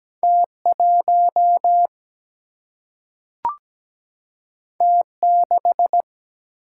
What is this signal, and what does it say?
Morse code, heard as a keyed tone, about 17 wpm, 700 Hz. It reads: T 1 T 6
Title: Single-Letter Single-Number 17wpm